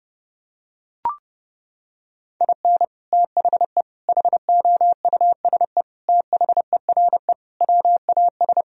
Morse code, heard as a keyed tone, about 30 wpm, 700 Hz. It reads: IN THE HOUSE THERE WAS
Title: Sentences from Top 1000 Words - Review Part 2 - 30wpm